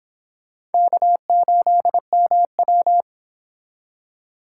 Morse code, heard as a keyed tone, about 32 wpm, 700 Hz. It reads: K8MW